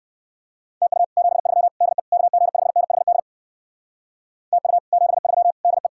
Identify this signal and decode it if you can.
Morse code, heard as a keyed tone, about 66 wpm, 700 Hz. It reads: TU 73 DE KG5NHD TU 73 DE